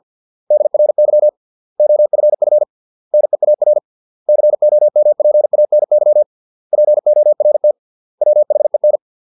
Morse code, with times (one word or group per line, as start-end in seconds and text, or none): 0.5-1.3 DRY
1.8-2.6 OFF
3.1-3.8 NEAR
4.3-6.2 COMPANY
6.7-7.7 PORT
8.2-9.0 WHEN